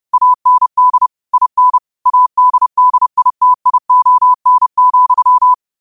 Morse code, heard as a keyed tone, about 30 wpm, 1000 Hz. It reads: AND IN ADDITION,
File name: qotdfiles / 30wpm